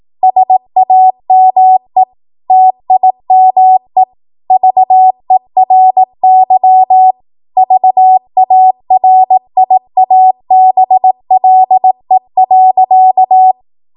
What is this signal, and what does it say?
Morse code, heard as a keyed tone, about 18 wpm, 750 Hz. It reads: SAME TIME VERY VARIABLE.
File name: Morse / 18WPM